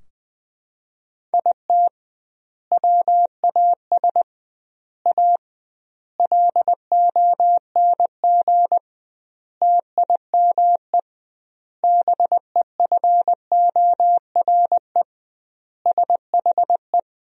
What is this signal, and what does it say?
Morse code, heard as a keyed tone, about 20 wpm, 700 Hz. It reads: IT WAS A LONG TIME BEFORE SHE
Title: 2x Word-Spacing - Sentences from Top 500 Words - 20wpm